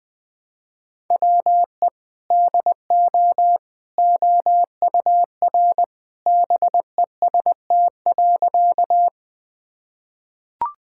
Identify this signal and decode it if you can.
Morse code, heard as a keyed tone, about 21 wpm, 700 Hz. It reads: WE DO OUR BEST.
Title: Sentences with 2-Word Phrases - 20wpm